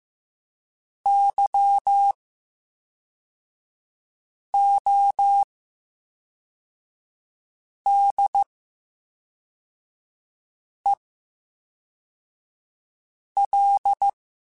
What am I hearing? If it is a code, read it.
Morse code, heard as a keyed tone, about 15 wpm, spaced as for 3 wpm, 800 Hz. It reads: YODEL
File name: Special INGs / Special INGs-mp3s / VerySlowToFaster Ing @10